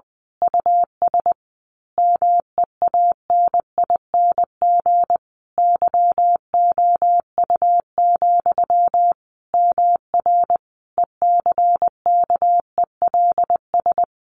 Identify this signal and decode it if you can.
Morse code, heard as a keyed tone, about 20 wpm, 700 Hz. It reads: US MEANING YOU, MR ECKELS
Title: A Sound Of Thunder 20wpm